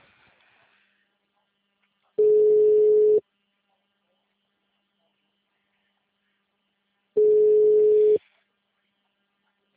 Ringback tone (Europe, 425 Hz).
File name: out- -2017- - - .53765